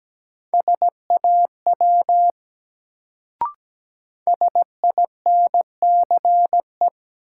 Morse code, heard as a keyed tone, about 17 wpm, 700 Hz. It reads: SAW SINCE
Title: Sentences from Top 300 Words - New Word Drill - 17wpm